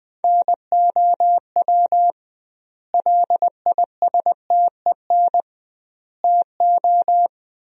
Morse code, heard as a keyed tone, about 21 wpm, 700 Hz. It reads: NOW LISTEN TO
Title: 2x Word-Spacing - Sentences from Top 400 Words - 20wpm